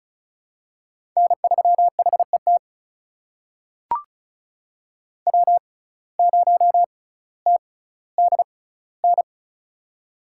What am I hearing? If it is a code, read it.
Morse code, heard as a keyed tone, about 36 wpm, spaced as for 18 wpm, 700 Hz. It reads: N3HET W0TDN
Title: USA Call Signs - 5 Characters - 6x Character Spacing - 35wpm